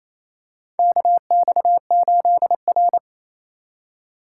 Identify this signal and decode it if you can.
Morse code, heard as a keyed tone, about 28 wpm, 700 Hz. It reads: KX8R
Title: USA Call Signs - 4 Characters - 4x Character Spacing - 28wpm